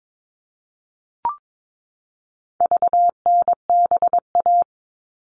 Morse code, heard as a keyed tone, about 22 wpm, 700 Hz. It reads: VNBA
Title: All Cut Numbers - 4-Digit 22wpm